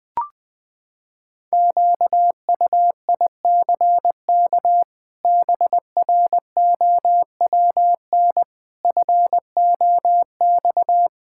Morse code, heard as a keyed tone, about 20 wpm, 700 Hz. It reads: QUICK BROWN FOX
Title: Sets of 8 Words - 20wpm